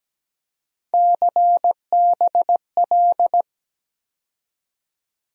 Morse code, heard as a keyed tone, about 17 wpm, 700 Hz. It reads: CBL
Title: USA Call Sign Postfixes - 3 Letters 17wpm